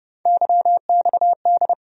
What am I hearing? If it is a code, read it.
Morse code, heard as a keyed tone, about 31 wpm, 700 Hz. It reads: YXD
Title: USA Call Sign Postfixes - 3 Letters 30wpm